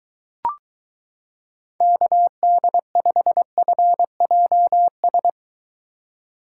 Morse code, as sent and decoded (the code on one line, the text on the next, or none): -.- -.. ..... ..-. .--- ...
KD5FJS